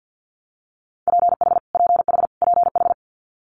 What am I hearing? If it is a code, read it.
Morse code, heard as a keyed tone, about 50 wpm, 690 Hz. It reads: PH PH PH
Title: Multi-letter Phonemes Repeated 3x 50wpm